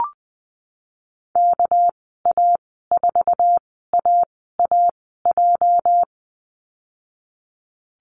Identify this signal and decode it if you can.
Morse code, heard as a keyed tone, about 20 wpm, spaced as for 15 wpm, 700 Hz. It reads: KA4AAJ
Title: 231 Lesson 37g - Letter x - Call Signs 20wpm